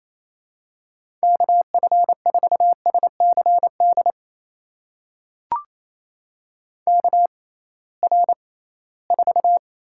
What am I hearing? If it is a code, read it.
Morse code, heard as a keyed tone, about 28 wpm, spaced as for 18 wpm, 700 Hz. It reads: KF4SCD KR4
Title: USA Call Signs - 6x Character Spacing - 28wpm